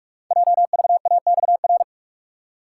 Morse code, heard as a keyed tone, about 44 wpm, 700 Hz. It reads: JVAXR
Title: Code Groups - 5 Letter:Numbers - 45wpm